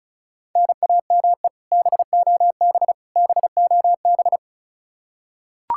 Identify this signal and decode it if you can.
Morse code, heard as a keyed tone, about 34 wpm, 700 Hz. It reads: NAME BOB BOB
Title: Practice QSO - Exchange 1 - Beginning - 35wpm